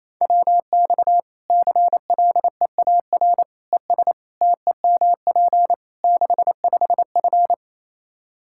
Morse code, heard as a keyed tone, about 28 wpm, 700 Hz. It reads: WX CLEAR ES TEMP 65F
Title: Practice QSO - Exchange 2 - Info - 28wpm